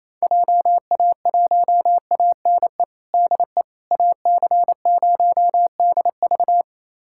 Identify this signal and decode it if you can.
Morse code, heard as a keyed tone, about 28 wpm, 700 Hz. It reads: JA1ANE DE AC0DV
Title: USA Callsign DE Callsign 28wpm